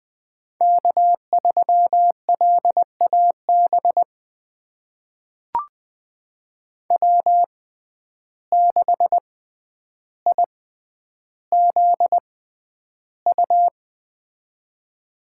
Morse code, heard as a keyed tone, about 20 wpm, spaced as for 10 wpm, 700 Hz. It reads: K3LAB W6IZU
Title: USA Call Signs - 5 Characters - 6x Character Spacing - 20wpm